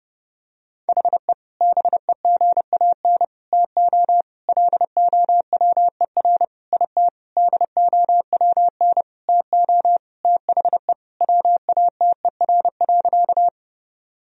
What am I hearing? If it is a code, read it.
Morse code, heard as a keyed tone, about 30 wpm, 700 Hz. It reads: HE BEGAN TO LOWER IT DOWN TO THE WATER.
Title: Sentences from Top 1000 Words - Review Part 4 - 30wpm